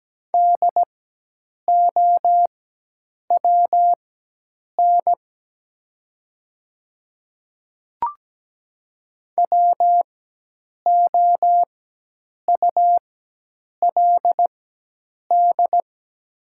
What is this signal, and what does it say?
Morse code, heard as a keyed tone, about 17 wpm, spaced as for 8 wpm, 700 Hz. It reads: DOWN WOULD
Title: 4x Character Spacing - Top 200 Words - 17wpm